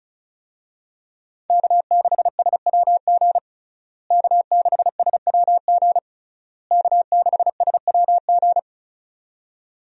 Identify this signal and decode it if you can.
Morse code, heard as a keyed tone, about 36 wpm, 700 Hz. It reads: K6SWG K6SWG K6SWG
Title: USA Call Signs - 5 Characters - Repeated 3x - 35wpm